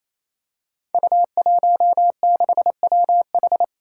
Morse code, heard as a keyed tone, about 27 wpm, 700 Hz. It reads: U16WH
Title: Five Challenging Letter-Number Combinations 28wpm